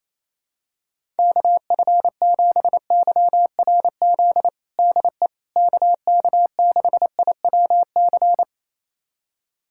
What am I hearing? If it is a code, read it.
Morse code, heard as a keyed tone, about 28 wpm, 700 Hz. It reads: KF7YRZ DE KK6IWC
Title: USA Callsign DE Callsign - 6 Characters - 28wpm